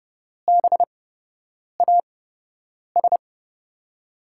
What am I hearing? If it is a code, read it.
Morse code, heard as a keyed tone, about 32 wpm, spaced as for 8 wpm, 700 Hz. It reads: BAS
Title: USA Call Signs - 5 Characters - 8x Character Spacing - 30wpm